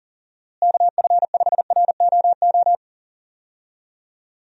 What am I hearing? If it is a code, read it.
Morse code, heard as a keyed tone, about 40 wpm, 700 Hz. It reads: KF5ROO
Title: USA Call Signs - 6 Characters - Repeated 3x - 40wpm